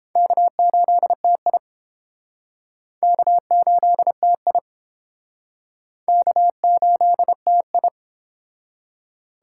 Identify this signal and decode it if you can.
Morse code, heard as a keyed tone, about 29 wpm, 700 Hz. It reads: K8TI K8TI K8TI